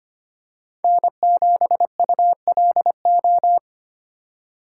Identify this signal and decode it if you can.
Morse code, heard as a keyed tone, about 25 wpm, 700 Hz. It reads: N7ULO
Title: USA Call Signs - 5 Characters - 3x Character Spacing - 25wpm